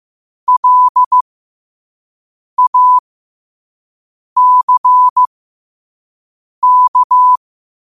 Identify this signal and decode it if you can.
Morse code, heard as a keyed tone, about 15 wpm, spaced as for 5 wpm, 1000 Hz. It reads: LACK
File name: qotdfiles / 5wpm